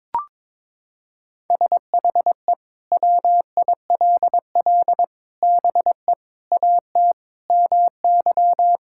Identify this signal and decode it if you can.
Morse code, heard as a keyed tone, about 22 wpm, 700 Hz. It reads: SHE WILL BE AT MY
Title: Sentences from Top 1600 Words 22wpm